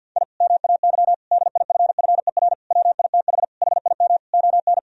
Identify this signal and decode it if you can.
Morse code, heard as a keyed tone, about 52 wpm, 700 Hz. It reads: I MAY DIFFER WITH HIM ON
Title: Sentences from Top 1000 Words - Review Part 1 - 50wpm